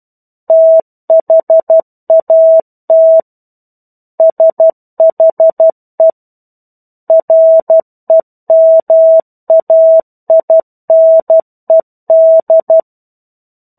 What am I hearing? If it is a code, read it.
Morse code, heard as a keyed tone, about 12 wpm, 650 Hz. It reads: THAT SHE REMAINED
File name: 01-Chapter-01 12 WPM